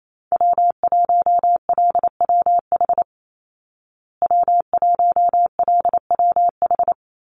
Morse code, heard as a keyed tone, about 28 wpm, 700 Hz. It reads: W1LWH W1LWH
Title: Call Signs Repeated 3x - 28wpm